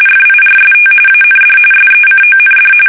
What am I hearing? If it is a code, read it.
RTTY, about 45.45 baud. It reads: FIRST AND SECOND
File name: fsk850 / usnavy1